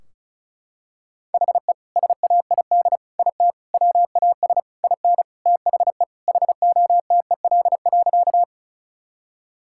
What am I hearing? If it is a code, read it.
Morse code, heard as a keyed tone, about 36 wpm, 700 Hz. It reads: HE SAID IT WAS IN THE HOTEL.
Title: Sentences from Top 1600 Words 35wpm